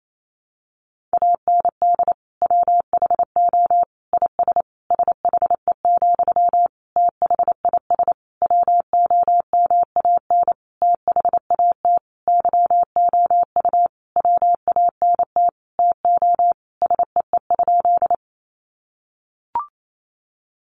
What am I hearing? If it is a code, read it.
Morse code, heard as a keyed tone, about 28 wpm, 700 Hz. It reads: AND WHO IS SHE, THIS WOMAN THAT YOU WANT TO SEE?
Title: Sentences from Top 1100 Words 28wpm